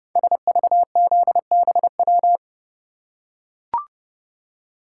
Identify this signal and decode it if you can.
Morse code, heard as a keyed tone, about 29 wpm, 690 Hz. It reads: SVZBW
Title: Code Groups - 5 Letters - 30wpm